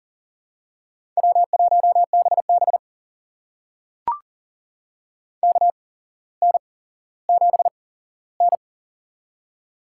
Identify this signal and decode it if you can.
Morse code, heard as a keyed tone, about 40 wpm, spaced as for 17 wpm, 700 Hz. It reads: W1BB KN7N
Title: USA Call Signs - 4 Characters - 8x Character Spacing - 40wpm